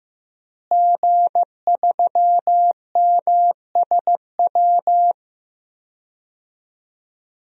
Morse code, heard as a keyed tone, about 15 wpm, 700 Hz. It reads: G3MSW
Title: DXCC-100 Call Signs - 5 Characters - Repeated 3x - 15wpm